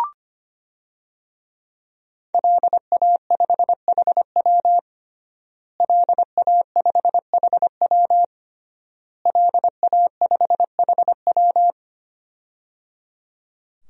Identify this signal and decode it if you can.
Morse code, heard as a keyed tone, about 25 wpm, 700 Hz. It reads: LA5HW LA5HW LA5HW
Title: DXCC-100 Call Signs - 5 Characters - Repeated 3x - 25wpm